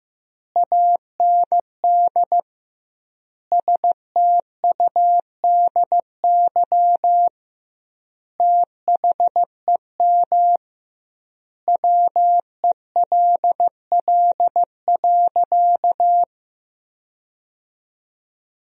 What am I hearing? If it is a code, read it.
Morse code, heard as a keyed tone, about 15 wpm, 700 Hz. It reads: AND STUDY THEM WELL.
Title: 2x Word-Spacing - Sentences from Top 300 Words - 15wpm